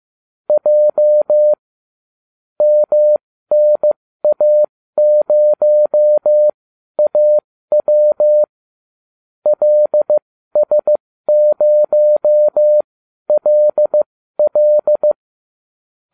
Morse code, heard as a keyed tone, about 15 wpm, spaced as for 12 wpm, 600 Hz. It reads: J MNA0AW LS0LL